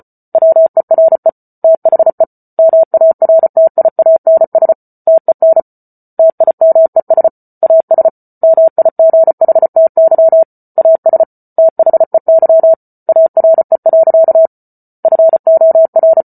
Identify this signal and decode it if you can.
Morse code, heard as a keyed tone, about 34 wpm, 650 Hz. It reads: WERE THE MARTIANS TEN TIMES AS MIGHTY AS THEY ARE. FOR
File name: Book Two / 08-Chapter 8 - 34 WPM